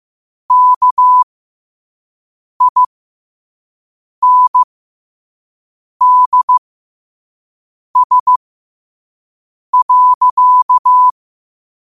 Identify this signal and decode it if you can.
Morse code, heard as a keyed tone, about 15 wpm, spaced as for 5 wpm, 1000 Hz. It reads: KINDS.
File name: qotdfiles / 5wpm